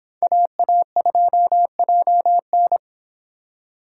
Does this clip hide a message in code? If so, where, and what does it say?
Yes: Morse code from 0.2 to 2.8 s, heard as a keyed tone, about 26 wpm, 700 Hz. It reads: AA2JN